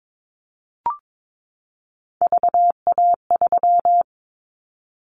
Morse code, heard as a keyed tone, about 22 wpm, 700 Hz. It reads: VA3
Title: Callsign Prefixes for Top 25 DXCC - 22wpm